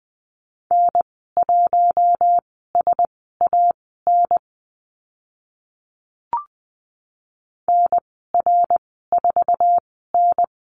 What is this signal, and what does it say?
Morse code, heard as a keyed tone, about 20 wpm, spaced as for 14 wpm, 700 Hz. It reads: N1SAN NR4N
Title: 055 Lesson 10g - Letter r - Call Signs 20wpm